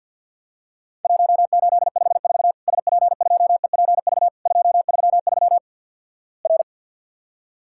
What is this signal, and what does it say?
Morse code, heard as a keyed tone, about 53 wpm, 700 Hz. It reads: 1854 SP2EPV 233 R
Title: Worked All Europe DX Contest - QTC Exchange - 50wpm